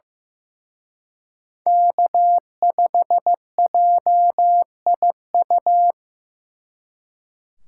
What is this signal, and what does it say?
Morse code, heard as a keyed tone, about 18 wpm, 700 Hz. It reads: K5JIU
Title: Speed Racing - USA 1x3 Call Signs 15wpm